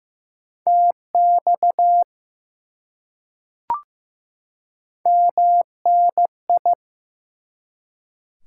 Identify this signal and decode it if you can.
Morse code, heard as a keyed tone, about 15 wpm, 700 Hz. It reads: TX MNI
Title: Top 180 QSO Elements 15wpm